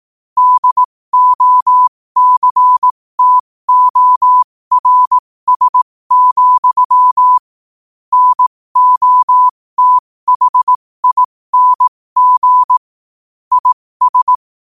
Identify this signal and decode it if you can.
Morse code, heard as a keyed tone, about 18 wpm, spaced as for 15 wpm, 1000 Hz. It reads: DOCTORS, NOTHING IS
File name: qotdfiles / 15wpm